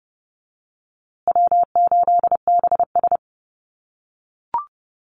Morse code, heard as a keyed tone, about 30 wpm, 700 Hz. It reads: W8BS
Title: USA Call Signs Repeated 2x 30wpm